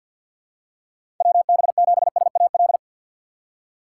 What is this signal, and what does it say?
Morse code, heard as a keyed tone, about 48 wpm, 700 Hz. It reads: WB7SAL
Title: USA Call Signs - 6 Characters - 3x Character Spacing - 50wpm